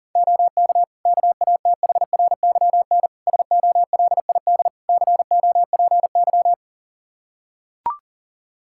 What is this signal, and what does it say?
Morse code, heard as a keyed tone, about 37 wpm, 700 Hz. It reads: OK KATHRYN SOLID COPY
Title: K1USN SST QSO Practice 40wpm